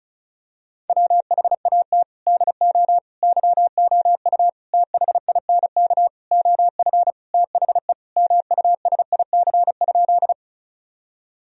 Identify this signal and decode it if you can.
Morse code, heard as a keyed tone, about 36 wpm, 700 Hz. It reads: WHAT DO YOU THINK OF THE MUSIC?
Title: Sentences from Top 1000 Words - Review Part 1 - 35wpm